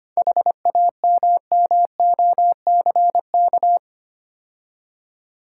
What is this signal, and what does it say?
Morse code, heard as a keyed tone, about 30 wpm, 700 Hz. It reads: HAMMOCK
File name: practice / 25wpm / Difficult Words - Speed-Racing with Hints - 25wpm